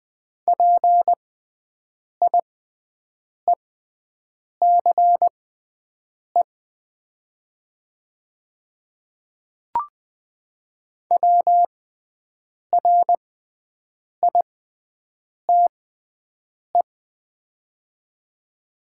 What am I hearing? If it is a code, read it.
Morse code, heard as a keyed tone, about 20 wpm, spaced as for 7 wpm, 700 Hz. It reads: PIECE WRITE
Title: 6x Character Spacing - Top 500 Words - 20wpm